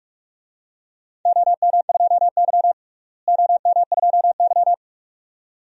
Morse code, heard as a keyed tone, about 47 wpm, 700 Hz. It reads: OM2Y OM2Y
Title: DXCC-100 Call Signs - 4 Characters - Repeated 2x - 45wpm